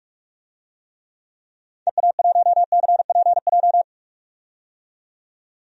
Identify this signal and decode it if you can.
Morse code, heard as a keyed tone, about 46 wpm, 700 Hz. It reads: EA1CPJ
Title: DXCC-100 Call Signs - 6 Characters - 45wpm